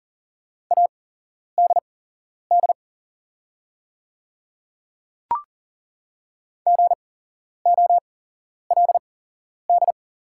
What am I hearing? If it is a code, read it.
Morse code, heard as a keyed tone, about 40 wpm, spaced as for 11 wpm, 700 Hz. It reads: ADD GOLD
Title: 8x Character Spacing - Top 500 Words - 40wpm